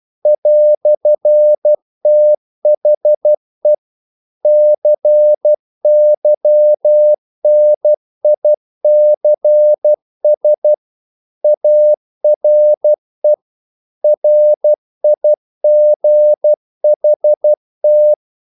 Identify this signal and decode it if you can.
Morse code, heard as a keyed tone, about 12 wpm, 600 Hz. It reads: "THE CYNICS ARE RIGHT
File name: CW / Practice / EnglishQuotations0312